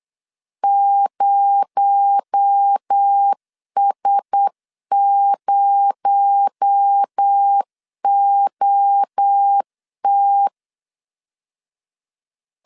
Morse code, heard as a keyed tone, about 8 wpm, 800 Hz. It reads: 0S0OT
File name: LEZIONI / 6 PKX0 / 6 PKX0 8wpm 3341